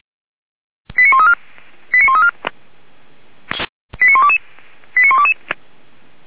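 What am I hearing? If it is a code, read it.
Five-tone selective call (ZVEI), heard as quick runs of five tones, tones of about 70 ms. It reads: 89134 89134 89122 89122
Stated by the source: SDR-received ZVEI five-tone selective call, code 89104.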